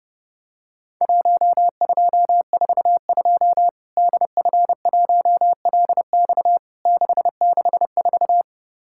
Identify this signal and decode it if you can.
Morse code, heard as a keyed tone, about 30 wpm, 700 Hz. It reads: 1242 DF1LX 664
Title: Worked All Europe DX Contest - QTC Exchange - 30wpm